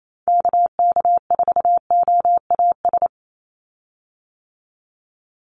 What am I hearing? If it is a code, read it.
Morse code, heard as a keyed tone, about 28 wpm, 700 Hz. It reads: KK4OAS
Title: USA Call Signs 28wpm